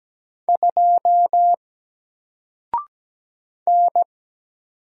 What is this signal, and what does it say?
Morse code, heard as a keyed tone, about 17 wpm, 700 Hz. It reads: 2 N
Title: Single Letter-Number 17wpm